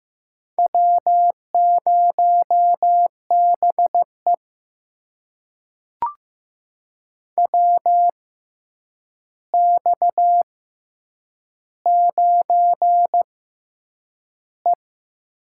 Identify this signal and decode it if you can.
Morse code, heard as a keyed tone, about 15 wpm, spaced as for 8 wpm, 700 Hz. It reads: W0BE WX9E
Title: USA Call Signs - 4 Characters - 6x Character Spacing - 15wpm